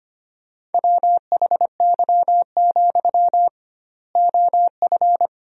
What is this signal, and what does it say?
Morse code, heard as a keyed tone, about 25 wpm, 700 Hz. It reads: WHY, OF
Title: 2x Word-Spacing - Sentences from Top 500 Words - 25wpm